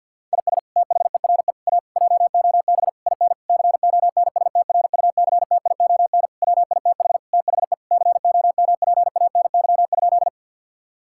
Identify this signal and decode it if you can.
Morse code, heard as a keyed tone, about 50 wpm, 700 Hz. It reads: IS THERE A JOB IN CONSTRUCTION WITH THE COMPANY?
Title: Sentences from Top 1600 Words 50wpm